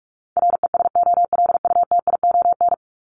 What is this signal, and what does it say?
Morse code, heard as a keyed tone, about 44 wpm, 700 Hz. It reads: RESOLUTION
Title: Top 1501-1750 Words 45wpm